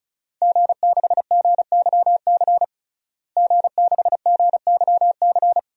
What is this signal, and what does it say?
Morse code, heard as a keyed tone, about 34 wpm, 700 Hz. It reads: G6GYC G6GYC
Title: DXCC-100 Call Signs - 5 Characters - Repeated 2x - 35wpm